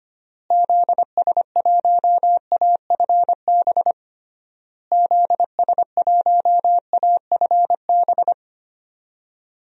Morse code, heard as a keyed tone, about 25 wpm, 700 Hz. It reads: ZS1AFB ZS1AFB
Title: DXCC-100 Call Signs - 6 Characters - Repeated 3x - 25wpm